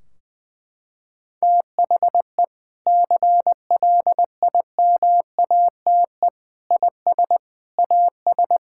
Morse code, heard as a keyed tone, about 20 wpm, 700 Hz. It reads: THE CLIMATE IS AS
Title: Sentences from Top 1600 Words 20wpm